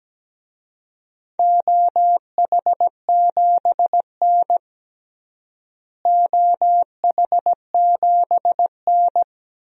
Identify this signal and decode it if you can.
Morse code, heard as a keyed tone, about 17 wpm, 700 Hz. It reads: OH7N OH7N
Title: DXCC-100 Call Signs - 4 Characters - Repeated 3x - 17wpm